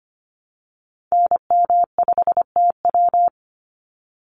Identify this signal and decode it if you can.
Morse code, heard as a keyed tone, about 25 wpm, 700 Hz. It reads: NM5TW
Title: NA SOTA Activator Callsigns 25wpm